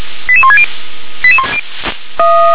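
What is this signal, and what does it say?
Five-tone selective call (ZVEI), heard as quick runs of five tones, tones of about 70 ms. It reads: 88160
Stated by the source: ZVEI five-tone selective call 88126 received on an SDR.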